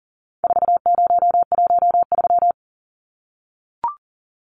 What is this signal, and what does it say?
Morse code, heard as a keyed tone, about 40 wpm, 700 Hz. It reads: 4013
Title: Numbers - 4-Digit 40wpm